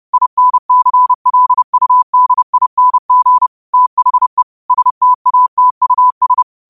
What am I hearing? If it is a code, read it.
Morse code, heard as a keyed tone, about 30 wpm, 1000 Hz. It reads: INCLUDING THE STATUS
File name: qotdfiles / 30wpm